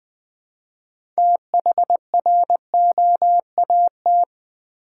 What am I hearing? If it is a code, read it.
Morse code, heard as a keyed tone, about 20 wpm, 700 Hz. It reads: THROAT
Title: 6 Letter Words - First 3 Letters Given - 20wpm